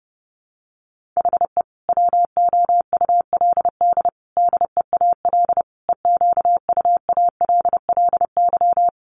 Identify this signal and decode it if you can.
Morse code, heard as a keyed tone, about 30 wpm, 700 Hz. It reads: HE WOULD DEAL EQUALLY
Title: Sentences from Top 1400 Words 30wpm